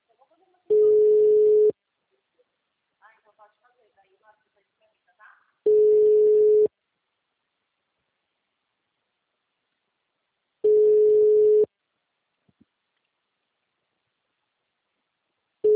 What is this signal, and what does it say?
Ringback tone (Europe, 425 Hz), with speech.